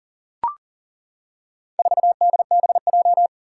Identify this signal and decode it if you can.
Morse code, heard as a keyed tone, about 40 wpm, 700 Hz. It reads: 4DBJ
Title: Four Challenging Letter-Number Combinations 40wpm